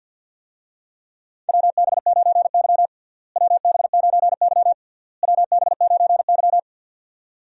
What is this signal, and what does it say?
Morse code, heard as a keyed tone, about 53 wpm, 700 Hz. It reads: WB9Y WB9Y WB9Y
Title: USA Call Signs - 4 Characters - Repeated 3x - 50wpm